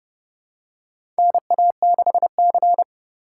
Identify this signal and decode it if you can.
Morse code, heard as a keyed tone, about 29 wpm, 700 Hz. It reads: NA6C
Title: USA Call Signs - 4 Characters - 2x Character Spacing - 30wpm